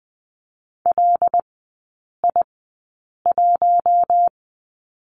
Morse code, heard as a keyed tone, about 20 wpm, spaced as for 8 wpm, 700 Hz. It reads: LI1
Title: 100 Lesson 17e - Letter u - Three Characters 20wpm